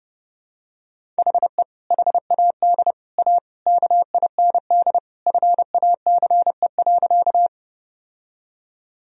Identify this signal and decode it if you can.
Morse code, heard as a keyed tone, about 30 wpm, 700 Hz. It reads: HE HAD A KIND FACE.
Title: Sentences from Top 1000 Words - Review Part 2 - 30wpm